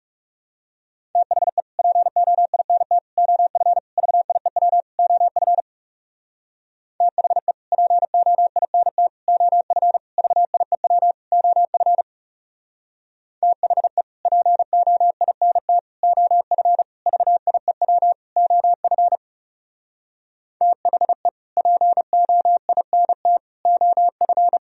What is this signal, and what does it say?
Morse code, heard as a keyed tone, about 37 wpm, 700 Hz. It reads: THE POINT OF VIEW OF THE POINT OF VIEW OF THE POINT OF VIEW OF THE POINT OF
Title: Speed-Racing - Sets of 5 Words 30wpm